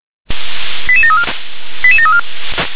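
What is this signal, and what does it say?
Five-tone selective call (ZVEI), heard as quick runs of five tones, tones of about 70 ms. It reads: 88734 88734
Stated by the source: ZVEI five-tone selective call 88083 received by SDR.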